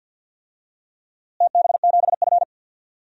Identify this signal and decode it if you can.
Morse code, heard as a keyed tone, about 48 wpm, 700 Hz. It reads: TB7F
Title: DXCC-100 Call Signs - 4 Characters - Repeated 2x - 50wpm